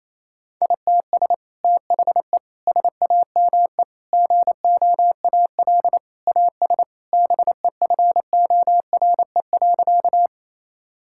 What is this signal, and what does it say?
Morse code, heard as a keyed tone, about 28 wpm, 700 Hz. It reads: ITS THE SAME GOAL AS BEFORE.
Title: Sentences from Top 1000 Words - Review Part 5 - 28wpm